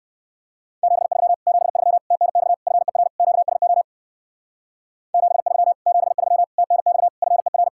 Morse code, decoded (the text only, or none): 73 73 TNX FER QSO 73 73 TNX FER